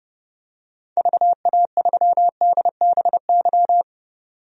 Morse code, heard as a keyed tone, about 30 wpm, 700 Hz. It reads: VA3DBY
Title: DXCC-100 Call Signs - 6 Characters - 30wpm